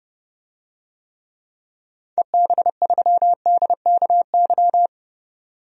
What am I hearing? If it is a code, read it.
Morse code, heard as a keyed tone, about 30 wpm, 700 Hz. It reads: EB3DKY
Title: DXCC-100 Call Signs - 6 Characters - 30wpm